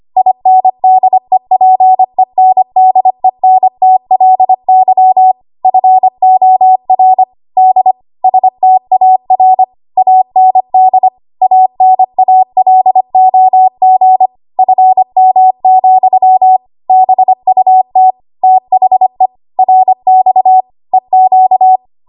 Morse code, heard as a keyed tone, about 25 wpm, 750 Hz. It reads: INDEPENDENTLY FOR D STAR AND ANALOG FM, BUT THE RX EQ
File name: Archive / 25 WPM / 25WPM